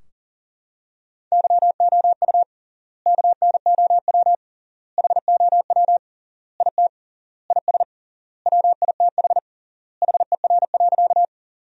Morse code, heard as a keyed tone, about 40 wpm, 700 Hz. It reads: YOU KNOW HOW IT IS WITH HER.
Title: 3x Word-Spacing - Sentences from Top 100 Words - 40wpm